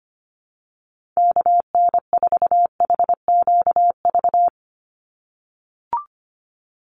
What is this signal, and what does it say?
Morse code, heard as a keyed tone, about 25 wpm, 700 Hz. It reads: KN4HQV
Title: USA Call Signs 25wpm